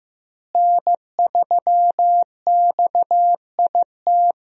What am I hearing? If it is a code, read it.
Morse code, heard as a keyed tone, about 15 wpm, 700 Hz. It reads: N3XIT
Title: USA Call Signs - 5 Characters - Repeated 3x - 15wpm